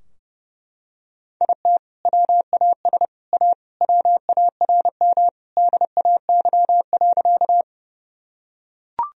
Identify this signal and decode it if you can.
Morse code, heard as a keyed tone, about 32 wpm, 700 Hz. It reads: IT WAS A WARM DAY.
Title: Sentences from Top 1000 Words - Review Part 2 - 30wpm